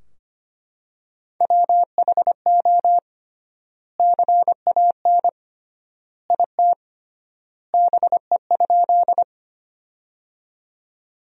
Morse code, heard as a keyed tone, about 25 wpm, 700 Hz. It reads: WHO CAN IT BE?
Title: 3x Word-Spacing - Sentences from Top 100 Words - 25wpm